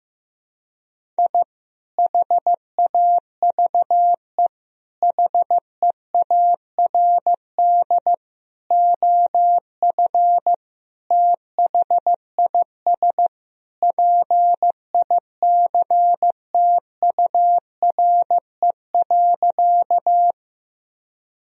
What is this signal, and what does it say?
Morse code, heard as a keyed tone, about 16 wpm, 700 Hz. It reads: I HAVE HEARD OF THIS PICTURE.
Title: Sentences from Top 1000 Words - Review Part 1 - 15wpm